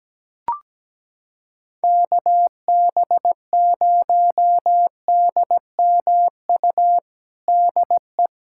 Morse code, heard as a keyed tone, about 17 wpm, 700 Hz. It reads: KB0DMU DE
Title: USA Callsign DE Callsign 17wpm